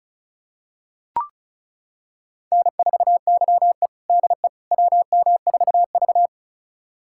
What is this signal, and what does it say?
Morse code, heard as a keyed tone, about 36 wpm, 690 Hz. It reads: N4YE DE WM4V
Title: USA Callsign DE Callsign - 4 Characters - 35wpm